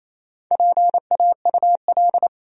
Morse code, heard as a keyed tone, about 28 wpm, 700 Hz. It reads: PAUL
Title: USA Common Names - Max 5 Letters - 28wpm